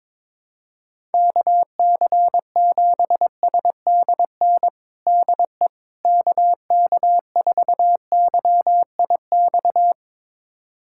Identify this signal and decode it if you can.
Morse code, heard as a keyed tone, about 22 wpm, 700 Hz. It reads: KC7SDN DE KK4YIX
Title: USA Callsign DE Callsign - 6 Characters - 22wpm